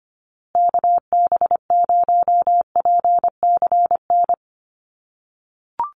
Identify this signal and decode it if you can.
Morse code, heard as a keyed tone, about 25 wpm, 700 Hz. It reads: KB0PCN
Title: USA Call Signs Repeated 2x 25wpm